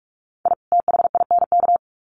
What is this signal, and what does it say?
Morse code, heard as a keyed tone, about 44 wpm, 700 Hz. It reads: I THINK
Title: Sets of 2 Words 45wpm